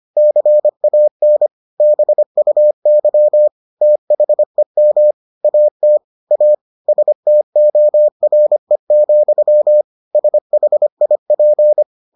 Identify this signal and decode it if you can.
Morse code, heard as a keyed tone, about 25 wpm, 600 Hz. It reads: CAN BUY THEM AT A STORE, SHIP